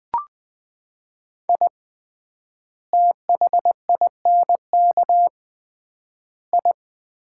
Morse code, heard as a keyed tone, about 21 wpm, 700 Hz. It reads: I THINK I
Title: 3x Word-Spacing - Sets of 3 Words - 20wpm